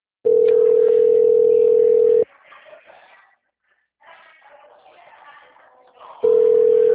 Ringback tone (North America), 440 + 480 Hz.